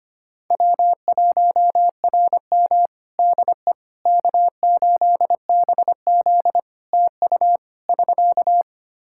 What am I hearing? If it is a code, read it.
Morse code, heard as a keyed tone, about 25 wpm, 700 Hz. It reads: W1RM DE K8BZ TU <SK>
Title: Practice QSO - Exchange 4 - Ending - 25wpm